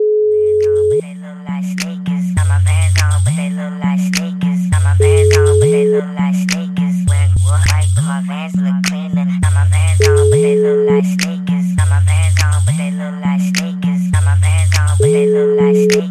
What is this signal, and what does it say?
Ringback tone (Europe, 425 Hz), with noise.